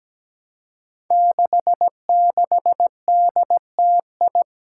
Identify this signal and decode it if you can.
Morse code, heard as a keyed tone, about 17 wpm, 700 Hz. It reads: 66DTI
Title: Five Challenging Letter-Number Combinations 17wpm